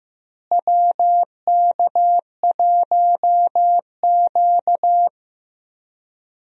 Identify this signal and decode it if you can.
Morse code, heard as a keyed tone, about 15 wpm, 700 Hz. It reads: WK1Q
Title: USA Call Signs - 4 Characters - Repeated 2x - 15wpm